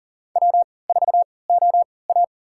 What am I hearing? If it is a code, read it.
Morse code, heard as a keyed tone, about 40 wpm, spaced as for 23 wpm, 700 Hz. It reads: W4OA
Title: USA Call Signs - 4 Characters - 3x Character Spacing - 40wpm